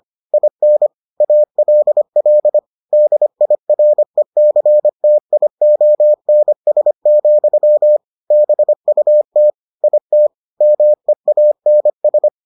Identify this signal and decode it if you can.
Morse code, heard as a keyed tone, about 25 wpm, 600 Hz. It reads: IN ALL DIRECTIONS, BUT IT MEANS